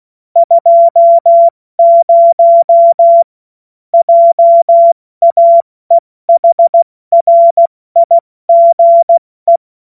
Morse code, heard as a keyed tone, about 16 wpm, 680 Hz. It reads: 20 JAEHRIGE
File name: Witz-08-Juli-2025-70-F80